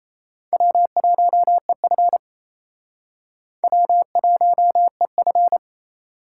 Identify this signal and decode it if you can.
Morse code, heard as a keyed tone, about 38 wpm, 700 Hz. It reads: W1EF W1EF